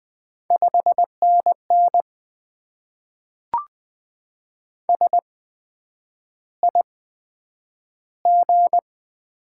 Morse code, heard as a keyed tone, about 20 wpm, spaced as for 9 wpm, 700 Hz. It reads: 5NN SIG
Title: Top 90 QSO Elements - 8x Character Spacing - 20wpm